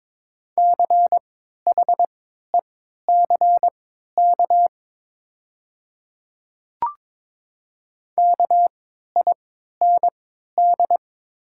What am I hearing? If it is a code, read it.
Morse code, heard as a keyed tone, about 22 wpm, spaced as for 12 wpm, 700 Hz. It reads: CHECK KIND
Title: 3x Character Spacing - Top 500 Words - 22wpm